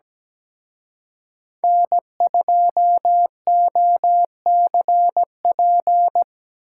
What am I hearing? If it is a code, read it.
Morse code, heard as a keyed tone, about 20 wpm, 700 Hz. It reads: N2OCP
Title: Speed Racing - USA Call Signs 17wpm